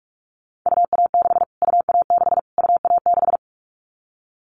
Morse code, heard as a keyed tone, about 46 wpm, 700 Hz. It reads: UA6 UA6 UA6
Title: Callsign Prefixes for Top 25 DXCC - Repeated 3x - 45wpm